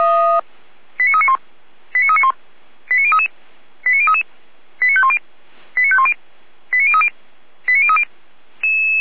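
Five-tone selective call (ZVEI), heard as quick runs of five tones, tones of about 70 ms. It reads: 89391 89391 89033 89033 89510 89510 89030 89030
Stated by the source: ZVEI five-tone selective call, received by SDR.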